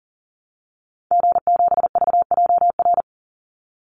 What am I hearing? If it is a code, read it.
Morse code, heard as a keyed tone, about 40 wpm, 700 Hz. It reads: G7VJR
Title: Call Signs Repeated 3x - 40wpm